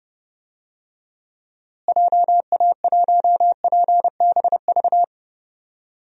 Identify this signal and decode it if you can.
Morse code, heard as a keyed tone, about 30 wpm, 700 Hz. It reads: JA1PBV
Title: DXCC-100 Call Signs - 6 Characters - Repeated 2x - 30wpm